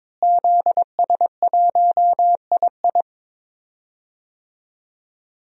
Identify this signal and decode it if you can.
Morse code, heard as a keyed tone, about 22 wpm, 700 Hz. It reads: ZS1II
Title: DXCC-100 Call Signs - 5 Characters - Repeated 3x - 22wpm